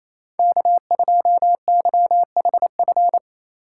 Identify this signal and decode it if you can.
Morse code, heard as a keyed tone, about 27 wpm, 700 Hz. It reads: K2YHF
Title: USA Call Signs - 5 Characters - 4x Character Spacing - 28wpm